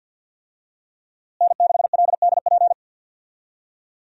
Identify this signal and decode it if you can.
Morse code, heard as a keyed tone, about 50 wpm, 700 Hz. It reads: N6LDP
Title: USA Call Signs - 5 Characters - Repeated 2x - 50wpm